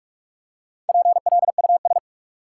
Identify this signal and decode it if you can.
Morse code, heard as a keyed tone, about 46 wpm, 700 Hz. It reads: PLUS
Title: Four Letter Words - 45wpm